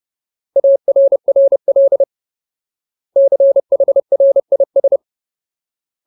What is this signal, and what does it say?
Morse code, heard as a keyed tone, about 30 wpm, 550 Hz. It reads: ARRL CHRIS
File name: Ultimate Word List / 30WPM Ultimate